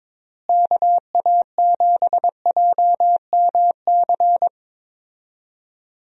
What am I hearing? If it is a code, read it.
Morse code, heard as a keyed tone, about 22 wpm, 700 Hz. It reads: KA7JMC
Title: USA Call Signs - 6 Characters - Repeated 2x - 22wpm